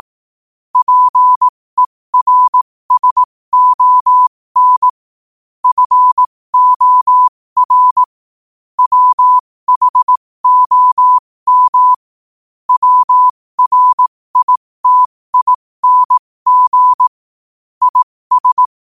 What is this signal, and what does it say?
Morse code, heard as a keyed tone, about 18 wpm, spaced as for 15 wpm, 1000 Hz. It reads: PERSON FOR WHOM WRITING IS
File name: qotdfiles / 15wpm